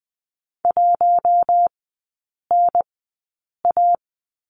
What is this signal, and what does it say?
Morse code, heard as a keyed tone, about 20 wpm, spaced as for 8 wpm, 700 Hz. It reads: 1NA
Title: 053 Lesson 10e - Letter r - Three Characters 20wpm